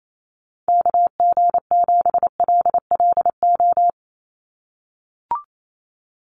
Morse code, heard as a keyed tone, about 28 wpm, 700 Hz. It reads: KG7LLO
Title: USA Call Signs 28wpm